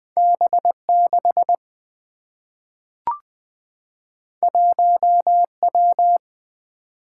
Morse code, heard as a keyed tone, about 20 wpm, 700 Hz. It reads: B 6 1 W
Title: Single-Letter Single-Number 20wpm